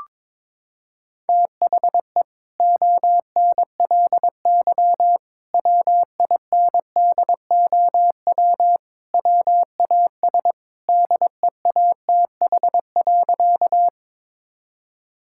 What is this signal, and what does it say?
Morse code, heard as a keyed tone, about 23 wpm, 700 Hz. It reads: THE ONLY WINDOW WAS DEATH.
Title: Sentences from Top 1600 Words 22wpm